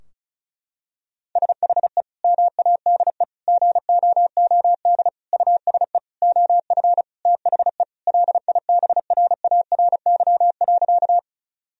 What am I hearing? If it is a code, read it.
Morse code, heard as a keyed tone, about 36 wpm, 700 Hz. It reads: SHE MADE GOOD USE OF THE LIBRARY.
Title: Sentences from Top 1000 Words - Review Part 4 - 35wpm